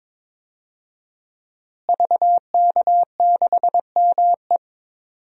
Morse code, heard as a keyed tone, about 21 wpm, 700 Hz. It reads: VK6ME
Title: DXCC-100 Call Signs - 5 Characters - Repeated 2x - 22wpm